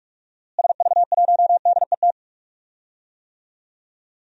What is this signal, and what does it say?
Morse code, heard as a keyed tone, about 45 wpm, 700 Hz. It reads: SV1DET
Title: DXCC-100 Call Signs - 6 Characters - Repeated 3x - 45wpm